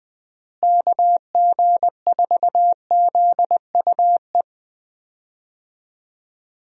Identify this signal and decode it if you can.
Morse code, heard as a keyed tone, about 20 wpm, 700 Hz. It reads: KG4ZUE
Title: USA Call Signs - 6 Characters - Repeated 2x - 20wpm